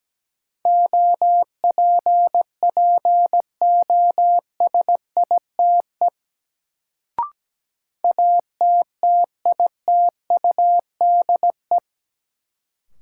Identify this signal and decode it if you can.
Morse code, heard as a keyed tone, about 17 wpm, 700 Hz. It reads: OPPOSITE ATTITUDE
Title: Sentences from Top 1300 Words - New Word Drill - 17wpm